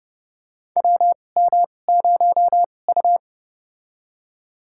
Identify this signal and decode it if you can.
Morse code, heard as a keyed tone, about 30 wpm, spaced as for 22 wpm, 700 Hz. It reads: WM0U
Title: USA Call Signs - 2x Character Spacing - 30wpm